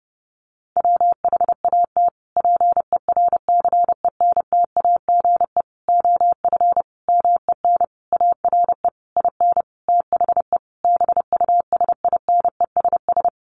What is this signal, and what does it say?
Morse code, heard as a keyed tone, about 30 wpm, 700 Hz. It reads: WHAT PERCENTAGE OF MEN ARE IN THE BUSINESS
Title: Sentences from Top 1300 Words 30wpm